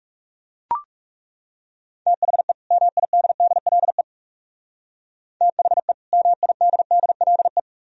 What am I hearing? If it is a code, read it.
Morse code, heard as a keyed tone, about 40 wpm, 700 Hz. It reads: THE MIDDLE THE MIDDLE